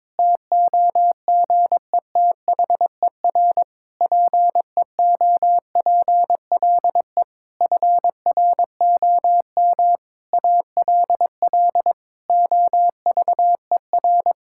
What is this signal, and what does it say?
Morse code, heard as a keyed tone, about 22 wpm, 700 Hz. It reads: TOGETHER PEOPLE FROM ALL OVER
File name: Difficult Sentences with Hints - 22wpm